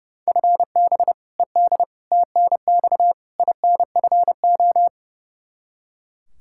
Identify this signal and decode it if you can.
Morse code, heard as a keyed tone, about 32 wpm, 700 Hz. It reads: FB ED TNX INFO
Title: K1USN SST QSO Practice 30wpm